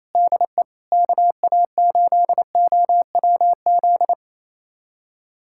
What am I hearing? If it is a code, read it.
Morse code, heard as a keyed tone, about 28 wpm, 700 Hz. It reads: DE KA8OWZ
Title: USA Callsign DE Callsign - 6 Characters - 28wpm